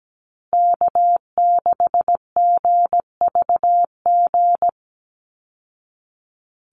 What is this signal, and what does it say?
Morse code, heard as a keyed tone, about 17 wpm, 700 Hz. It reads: K6GVG
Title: Call Signs Repeated 3x - 17wpm